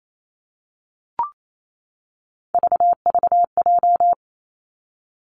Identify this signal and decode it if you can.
Morse code, heard as a keyed tone, about 28 wpm, 700 Hz. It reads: VVJ
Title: Three Challenging Letter-Number Combinations 28wpm